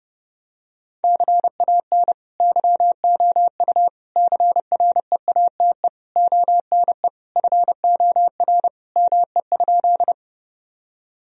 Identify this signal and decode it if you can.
Morse code, heard as a keyed tone, about 32 wpm, 700 Hz. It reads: CAN YOU CREATE ONE FOR ME?
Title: Sentences from Top 1000 Words - Review Part 3 - 30wpm